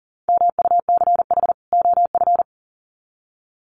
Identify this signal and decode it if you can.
Morse code, heard as a keyed tone, about 40 wpm, 700 Hz. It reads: MUCH OF
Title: Sets of 2 Words 40wpm